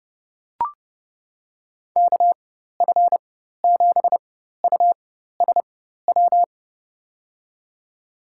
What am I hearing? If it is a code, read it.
Morse code, heard as a keyed tone, about 30 wpm, spaced as for 14 wpm, 700 Hz. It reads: KF7USW
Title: USA Call Signs - 6 Characters - 4x Character Spacing - 30wpm